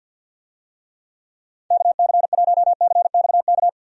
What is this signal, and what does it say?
Morse code, heard as a keyed tone, about 46 wpm, 700 Hz. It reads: KC1CXK
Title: USA Call Signs - 6 Characters - 50wpm